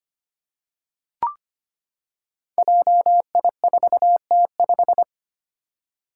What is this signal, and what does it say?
Morse code, heard as a keyed tone, about 25 wpm, 700 Hz. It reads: JI4T5
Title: Five Challenging Letter-Number Combinations 25wpm